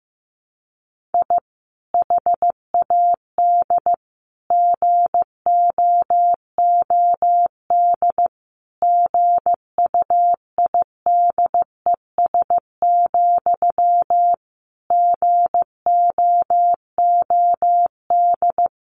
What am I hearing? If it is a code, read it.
Morse code, heard as a keyed tone, about 15 wpm, 700 Hz. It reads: I HAD GOOD GUIDES, GOOD
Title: A Sound Of Thunder 15wpm